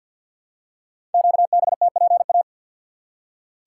Morse code, heard as a keyed tone, about 48 wpm, 700 Hz. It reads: QBTPA
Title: Code Groups - 5 Letters - 50wpm